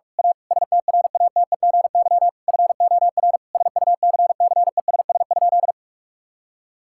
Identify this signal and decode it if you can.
Morse code, heard as a keyed tone, about 46 wpm, 700 Hz. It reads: A STRATEGY FOR SUCCESS?
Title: Sentences from Top 1000 Words - Review Part 5 - 45wpm